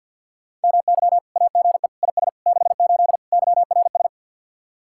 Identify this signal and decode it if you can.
Morse code, heard as a keyed tone, about 65 wpm, spaced as for 57 wpm, 700 Hz. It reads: MY AGE IS 68 YRS
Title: Speed-Racing Common QSO Phrases Encore - 50wpm